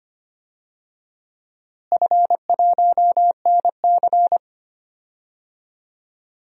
Morse code, heard as a keyed tone, about 25 wpm, 700 Hz. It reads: F1NC
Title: DXCC-100 Call Signs - 4 Characters - Repeated 2x - 25wpm